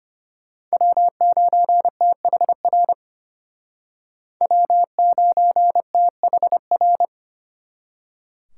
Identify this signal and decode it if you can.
Morse code, heard as a keyed tone, about 32 wpm, 700 Hz. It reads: W9THR W9THR